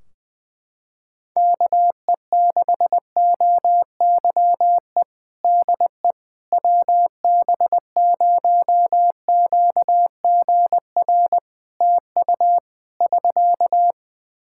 Morse code, heard as a keyed tone, about 20 wpm, 700 Hz. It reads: KE6OYE DE WB0QGR TU <SK>
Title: Practice QSO - Exchange 4 - Ending - 20wpm